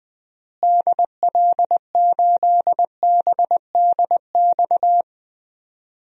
Morse code, heard as a keyed tone, about 20 wpm, 700 Hz. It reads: DL8BDX
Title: DXCC-100 Call Signs - 6 Characters - Repeated 3x - 20wpm